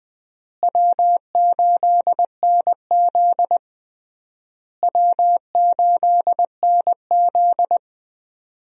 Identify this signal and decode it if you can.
Morse code, heard as a keyed tone, about 21 wpm, 700 Hz. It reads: W8NZ W8NZ
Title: USA Call Signs - 4 Characters - Repeated 3x - 20wpm